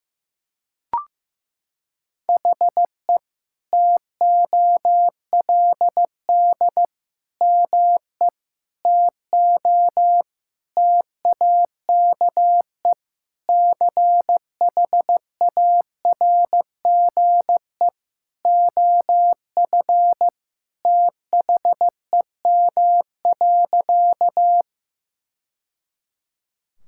Morse code, heard as a keyed tone, about 15 wpm, 700 Hz. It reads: HE TOLD ME TO TAKE CHARGE OF THEM.
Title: Sentences from Top 1000 Words - Review Part 6 - 15wpm